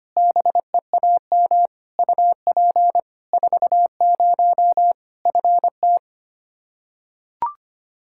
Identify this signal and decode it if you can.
Morse code, heard as a keyed tone, about 25 wpm, 700 Hz. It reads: BEAM UP 40 FT
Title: Practice QSO - Exchange 2 - Info - 25wpm